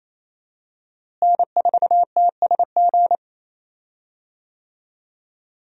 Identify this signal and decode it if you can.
Morse code, heard as a keyed tone, about 28 wpm, 700 Hz. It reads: N4TSG
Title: USA Call Signs - 5 Characters - Repeated 3x - 28wpm